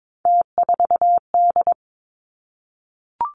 Morse code, heard as a keyed tone, about 22 wpm, 700 Hz. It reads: T4D
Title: Three Challenging Letter-Number Combinations 22wpm